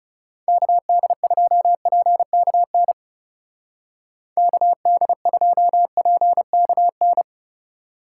Morse code, heard as a keyed tone, about 38 wpm, 700 Hz. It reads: KD2PKN KD2PKN